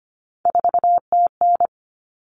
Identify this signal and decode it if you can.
Morse code, heard as a keyed tone, about 25 wpm, 700 Hz. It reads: 4TN
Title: Common Cut Numbers - 3-Digit 25wpm